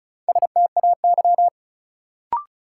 Morse code, heard as a keyed tone, about 36 wpm, 700 Hz. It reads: STAY
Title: Sentences from Top 500 Words - New Word Drill - 35wpm